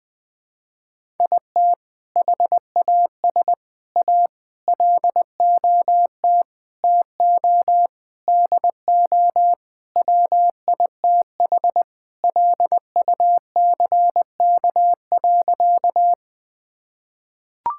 Morse code, heard as a keyed tone, about 19 wpm, 700 Hz. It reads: IT HAS A LOT TO DO WITH LUCK.
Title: Sentences with 5-Word Phrases - 20wpm